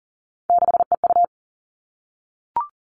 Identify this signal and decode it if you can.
Morse code, heard as a keyed tone, about 40 wpm, 700 Hz. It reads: 6EU
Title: All Cut Numbers - 3-Digit 40wpm